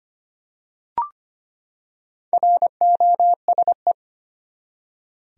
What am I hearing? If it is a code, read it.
Morse code, heard as a keyed tone, about 25 wpm, 700 Hz. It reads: ROSE
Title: USA Common Names - Max 4 Letters - 25wpm